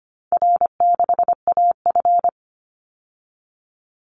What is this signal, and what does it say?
Morse code, heard as a keyed tone, about 25 wpm, 700 Hz. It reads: R6AF
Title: Call Signs Repeated 2x - 25wpm